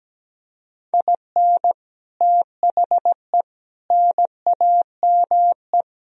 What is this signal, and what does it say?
Morse code, heard as a keyed tone, about 17 wpm, 700 Hz. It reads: IN THE NAME
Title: Speed-Racing - Sets of 4 Words 17wpm